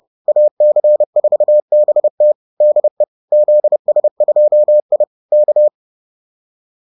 Morse code, heard as a keyed tone, about 30 wpm, 600 Hz. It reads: AC4BT DE ZS2I K